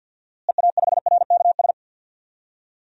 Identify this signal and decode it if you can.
Morse code, heard as a keyed tone, about 52 wpm, 700 Hz. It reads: EA5RKS
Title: DXCC-100 Call Signs - 6 Characters - Repeated 2x - 50wpm